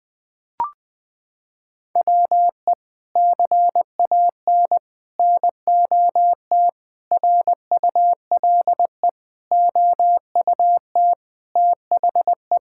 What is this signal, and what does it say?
Morse code, heard as a keyed tone, about 20 wpm, 700 Hz. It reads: WE CAN NOT RULE OUT THE
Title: Sets of 8 Words - 20wpm